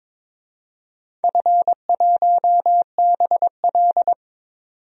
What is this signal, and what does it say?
Morse code, heard as a keyed tone, about 22 wpm, 700 Hz. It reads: F1BL
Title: DXCC-100 Call Signs - 4 Characters - 22wpm